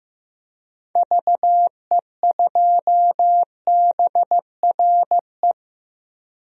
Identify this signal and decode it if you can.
Morse code, heard as a keyed tone, about 15 wpm, 700 Hz. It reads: VE2BRE
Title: DXCC-100 Call Signs - 6 Characters - Repeated 3x - 15wpm